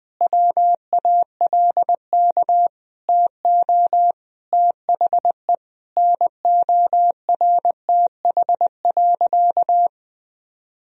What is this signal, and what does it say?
Morse code, heard as a keyed tone, about 20 wpm, 700 Hz. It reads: WALK TO THE NORTH.
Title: Sentences from Top 1000 Words - Review Part 1 - 20wpm